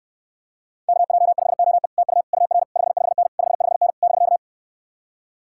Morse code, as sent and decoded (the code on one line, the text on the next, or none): -.- .--- ..... --- . / - ..- / ..- .-. / ..... ..... -. / ..... ..... -. / -...-.-
KJ5OE TU UR 55N 55N <BK>